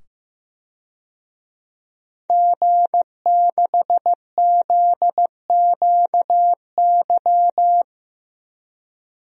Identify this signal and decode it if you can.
Morse code, heard as a keyed tone, about 15 wpm, 700 Hz. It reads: G6ZQY
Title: DXCC-100 Call Signs - 5 Characters - 15wpm